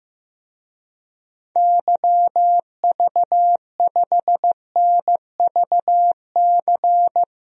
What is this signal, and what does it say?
Morse code, heard as a keyed tone, about 15 wpm, 700 Hz. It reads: YV5NVC
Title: DXCC-100 Call Signs - 6 Characters - Repeated 3x - 15wpm